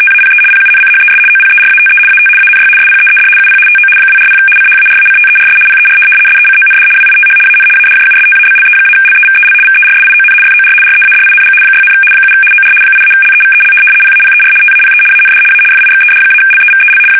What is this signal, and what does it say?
RTTY, about 75 baud. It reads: TAPE PUNCHES OPERATE AT ANY SPEED UP TO 240 CHARACTERS PER SECOND WITH AN ADJUSTABLE-LEVEL VERSION FOR READING 6, 7 OR 8-LEVEL PAPER TAPE AND A FIXED-LEVEL
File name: fsk850-75B / teletype-corp